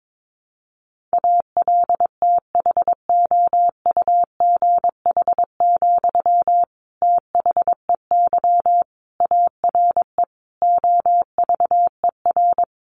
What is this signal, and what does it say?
Morse code, heard as a keyed tone, about 22 wpm, 700 Hz. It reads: ALTHOUGH, THEY ARE OVER
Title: Sentences from Top 600 Words 22wpm